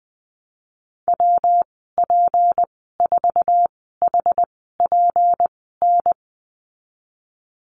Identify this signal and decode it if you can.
Morse code, heard as a keyed tone, about 20 wpm, spaced as for 15 wpm, 700 Hz. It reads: WP4HPN
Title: 167 Lesson 27g - Number 7 - Call Signs 20wpm